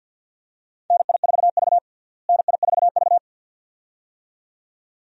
Morse code, heard as a keyed tone, about 53 wpm, 700 Hz. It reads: NI4V NI4V
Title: USA Call Signs - 4 Characters - Repeated 2x - 50wpm